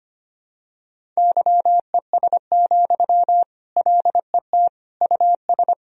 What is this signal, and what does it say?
Morse code, heard as a keyed tone, about 25 wpm, 700 Hz. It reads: YES, LET US
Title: Sentences from Top 1000 Words - Review Part 1 - 25wpm